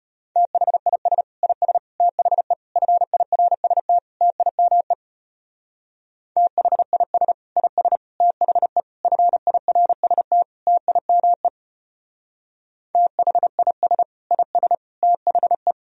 Morse code, heard as a keyed tone, about 30 wpm, 700 Hz. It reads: THIS IS THE FIRST TIME THIS IS THE FIRST TIME THIS IS THE
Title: Speed-Racing - Sets of 5 Words 25wpm